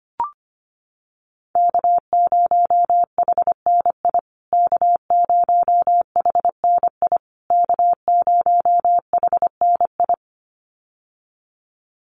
Morse code, heard as a keyed tone, about 25 wpm, 700 Hz. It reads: K0HNI K0HNI K0HNI
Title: USA Call Signs Repeated 3x 25wpm